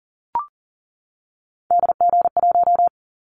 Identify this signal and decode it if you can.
Morse code, heard as a keyed tone, about 40 wpm, 700 Hz. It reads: DG1
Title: Callsign Prefixes for Top 25 DXCC - 40wpm